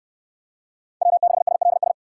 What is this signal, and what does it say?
Morse code, heard as a keyed tone, about 69 wpm, 700 Hz. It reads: W6SLS